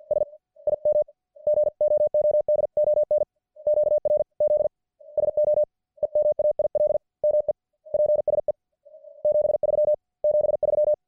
Morse code, heard as a keyed tone, about 50 wpm, 590 Hz. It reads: S IM GOOD ON QRZ SO EMAIL ME PSE 73 73